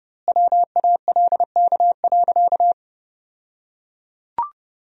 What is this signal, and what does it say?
Morse code, heard as a keyed tone, about 30 wpm, 700 Hz. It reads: WALK.
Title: 2x Word-Spacing - Sentences from Top 300 Words - 30wpm